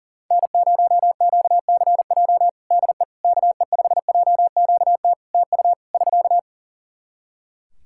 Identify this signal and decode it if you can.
Morse code, heard as a keyed tone, about 40 wpm, 700 Hz. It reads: N0QCJ DE KE5JQT TU <SK>
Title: Practice QSO - Exchange 4 - Ending - 40wpm